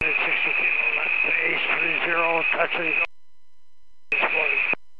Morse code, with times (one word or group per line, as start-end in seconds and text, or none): none